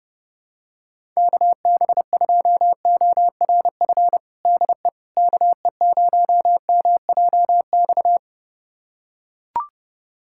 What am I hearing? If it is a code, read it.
Morse code, heard as a keyed tone, about 30 wpm, 700 Hz. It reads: KB2ORF DE KE0MJX
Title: USA Callsign DE Callsign - 6 Characters - 30wpm